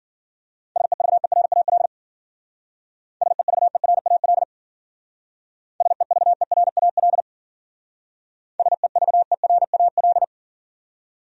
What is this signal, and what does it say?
Morse code, heard as a keyed tone, about 48 wpm, 700 Hz. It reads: SEVERAL SEVERAL SEVERAL SEVERAL
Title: Speed-Racing Top 500 Words 40wpm